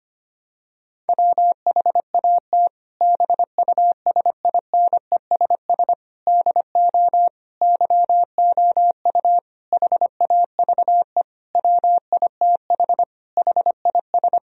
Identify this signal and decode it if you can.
Morse code, heard as a keyed tone, about 25 wpm, 700 Hz. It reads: WHAT BUSINESS DO YOU HAVE WITH HIS
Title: Sentences from Top 1000 Words - Review Part 3 - 25wpm